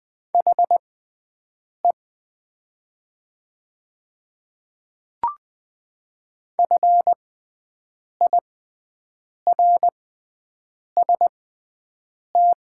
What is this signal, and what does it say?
Morse code, heard as a keyed tone, about 20 wpm, spaced as for 7 wpm, 700 Hz. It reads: HE FIRST
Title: 6x Character Spacing - Top 300 Words - 20wpm